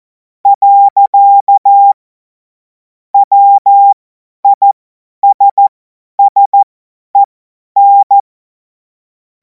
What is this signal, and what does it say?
Morse code, heard as a keyed tone, about 14 wpm, spaced as for 10 wpm, 800 Hz. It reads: . WISSEN
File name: Morsekurs / Witze / Witz 17. März 2026 50 F70